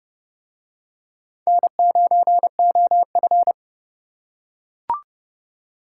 Morse code, heard as a keyed tone, about 28 wpm, 700 Hz. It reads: N9OF